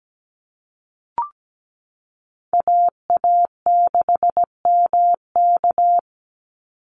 Morse code, heard as a keyed tone, about 17 wpm, 700 Hz. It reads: AA6MK
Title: Call Signs Repeated 3x - 17wpm